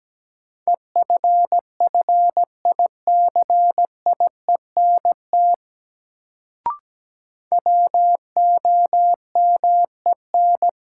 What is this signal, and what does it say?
Morse code, heard as a keyed tone, about 17 wpm, 700 Hz. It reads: EFFICIENT WOMEN
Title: Sentences from Top 1300 Words - New Word Drill Alt - 17wpm